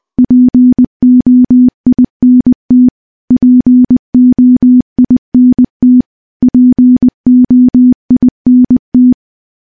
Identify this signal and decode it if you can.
Morse code, heard as a keyed tone, about 20 wpm, 250 Hz. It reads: POINT POINT POINT
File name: morse-training-20WPM